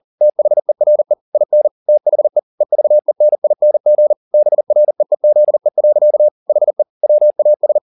Morse code, heard as a keyed tone, about 40 wpm, 600 Hz. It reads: THERE IN THE EVENING BREEZE. HE WAS